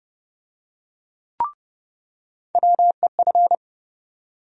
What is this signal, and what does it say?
Morse code, heard as a keyed tone, about 30 wpm, 700 Hz. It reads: WEF
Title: USA Call Sign Postfixes - 3 Letters 30wpm